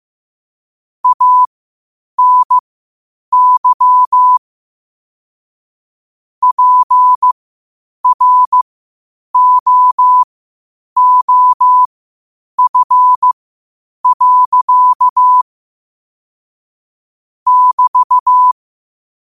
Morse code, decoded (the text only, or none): ANY PROOF. <BT>